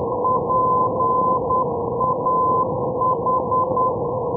Morse code, heard as a keyed tone, about 10 wpm, 1020 Hz, with noise. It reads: PAH
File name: europe / d-2817 ndb schwerin-parchim pah-315-hh-2012-03-rb